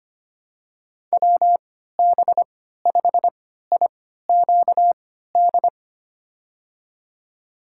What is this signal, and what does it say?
Morse code, heard as a keyed tone, about 25 wpm, spaced as for 14 wpm, 700 Hz. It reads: WB5IQD
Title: USA Call Signs - 6 Characters - 3x Character Spacing - 25wpm